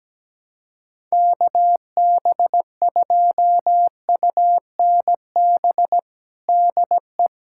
Morse code, heard as a keyed tone, about 17 wpm, 700 Hz. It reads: KB2UNB DE
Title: USA Callsign DE Callsign - 6 Characters - 17wpm